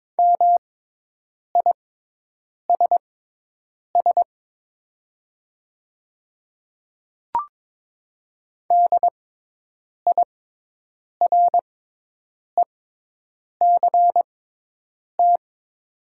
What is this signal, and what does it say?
Morse code, heard as a keyed tone, about 22 wpm, spaced as for 8 wpm, 700 Hz. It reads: MISS DIRECT
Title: 6x Character Spacing - Top 500 Words - 22wpm